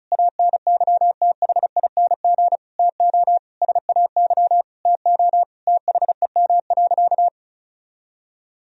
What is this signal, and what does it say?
Morse code, heard as a keyed tone, about 34 wpm, 700 Hz. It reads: ANYTHING TO SAY TO THEM.
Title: Sentences from Top 1000 Words - Review Part 1 - 35wpm